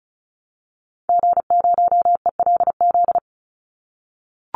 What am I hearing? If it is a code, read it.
Morse code, heard as a keyed tone, about 34 wpm, 700 Hz. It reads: G0ELZ
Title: Call Signs - 35wpm